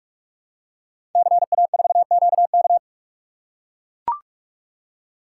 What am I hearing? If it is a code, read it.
Morse code, heard as a keyed tone, about 44 wpm, 700 Hz. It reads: CA4QK
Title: Code Groups - 5 Letter:Numbers - 45wpm